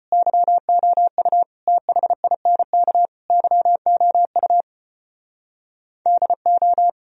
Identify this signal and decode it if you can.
Morse code, heard as a keyed tone, about 33 wpm, 700 Hz. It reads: YOU THINK YOU DO